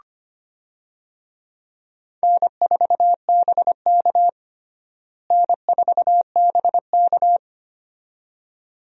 Morse code, heard as a keyed tone, about 25 wpm, 700 Hz. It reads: N4BK N4BK
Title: USA Call Signs - 4 Characters - Repeated 2x - 25wpm